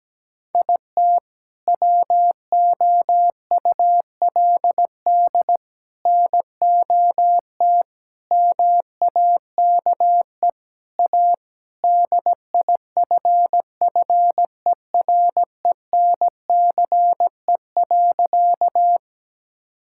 Morse code, heard as a keyed tone, about 17 wpm, 700 Hz. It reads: IT WOULD NOT MAKE A DIFFERENCE.
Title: Sentences from Top 1000 Words - Review Part 5 - 17wpm